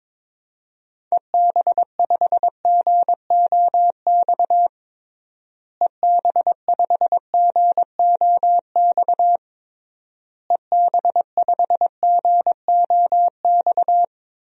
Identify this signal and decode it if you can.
Morse code, heard as a keyed tone, about 22 wpm, 700 Hz. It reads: EB5GOX EB5GOX EB5GOX
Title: DXCC-100 Call Signs - 6 Characters - Repeated 3x - 22wpm